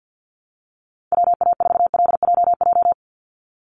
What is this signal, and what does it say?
Morse code, heard as a keyed tone, about 48 wpm, 700 Hz. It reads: WA4LJJ
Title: Call Signs - 50wpm